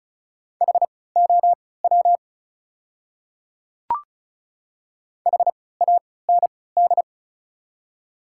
Morse code, heard as a keyed tone, about 34 wpm, spaced as for 20 wpm, 700 Hz. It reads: HOW HAND
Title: 3x Character Spacing - Top 200 Words - 35wpm